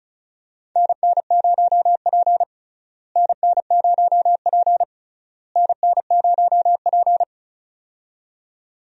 Morse code, heard as a keyed tone, about 34 wpm, 700 Hz. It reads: NN0P NN0P NN0P
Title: USA Call Signs - 4 Characters - Repeated 3x - 35wpm